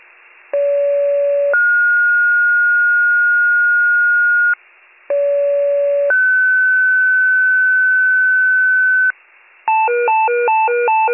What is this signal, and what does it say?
Two-tone page, heard as a one-second tone, then a three-second tone, about 570 then 1430 Hz, about 570 then 1500 Hz. It is heard from 0.5 to 9.1 s.